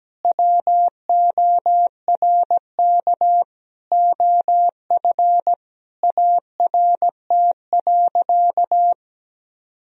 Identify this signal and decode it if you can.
Morse code, heard as a keyed tone, about 17 wpm, 700 Hz. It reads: WORK OF ART.
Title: Sentences from Top 1000 Words - Review Part 4 - 17wpm